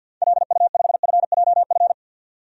Morse code, heard as a keyed tone, about 50 wpm, 700 Hz. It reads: LU5FJF
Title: DXCC-100 Call Signs - 6 Characters - 50wpm